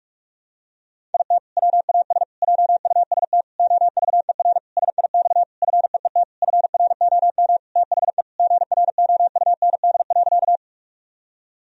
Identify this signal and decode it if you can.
Morse code, heard as a keyed tone, about 41 wpm, 700 Hz. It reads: IT WAS JUST OVER SIX FEET FROM THE GROUND.
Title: Sentences from Top 1000 Words - Review Part 2 - 45wpm